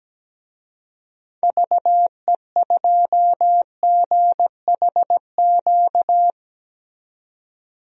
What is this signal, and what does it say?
Morse code, heard as a keyed tone, about 17 wpm, 700 Hz. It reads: VE2GHQ
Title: DXCC-100 Call Signs - 6 Characters - 17wpm